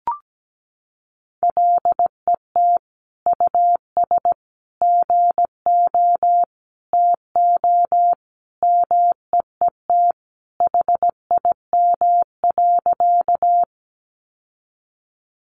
Morse code, heard as a keyed tone, about 17 wpm, 700 Hz. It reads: LET US GO TO MEET HIM.
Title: Sentences from Top 1100 Words 17wpm